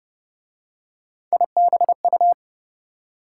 Morse code, heard as a keyed tone, about 33 wpm, 700 Hz. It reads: IBU